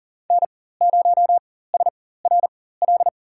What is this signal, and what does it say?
Morse code, heard as a keyed tone, about 40 wpm, spaced as for 19 wpm, 700 Hz. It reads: N0SRL
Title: USA Call Signs - 5 Characters - 4x Character Spacing - 40wpm